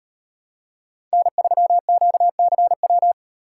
Morse code, heard as a keyed tone, about 38 wpm, 700 Hz. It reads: N3QCW